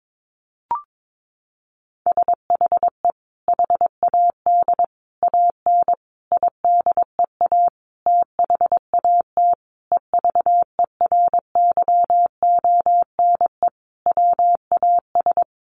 Morse code, heard as a keyed tone, about 22 wpm, 700 Hz. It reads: SHE HAD AN IDEA THAT EVERYONE WAS
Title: Sentences from Top 1400 Words 22wpm